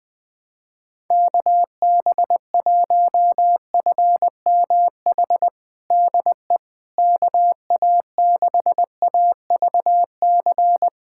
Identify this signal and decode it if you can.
Morse code, heard as a keyed tone, about 20 wpm, 700 Hz. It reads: KB1FMH DE KA6AVC
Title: USA Callsign DE Callsign - 6 Characters - 20wpm